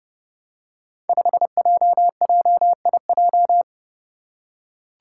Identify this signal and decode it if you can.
Morse code, heard as a keyed tone, about 32 wpm, 700 Hz. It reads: 5JJIJ
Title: Five Challenging Letter-Number Combinations 30wpm